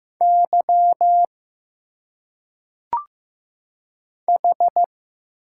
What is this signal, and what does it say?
Morse code, heard as a keyed tone, about 15 wpm, 700 Hz. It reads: Y H
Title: Single Letters - Phonetic Pronunciation 15wpm